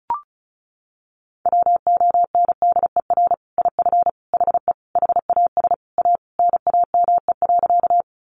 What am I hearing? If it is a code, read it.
Morse code, heard as a keyed tone, about 34 wpm, 700 Hz. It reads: WONDER IF HE HAS A NAME.
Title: Sentences from Top 500 Words 35wpm